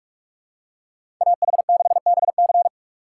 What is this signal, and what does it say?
Morse code, heard as a keyed tone, about 45 wpm, 700 Hz. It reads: AH6BC
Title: USA Call Signs - 2x Character Spacing - 45wpm